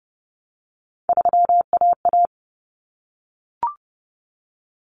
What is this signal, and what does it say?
Morse code, heard as a keyed tone, about 30 wpm, 700 Hz. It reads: 3AA
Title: Common Cut Numbers - 3-Digit 30wpm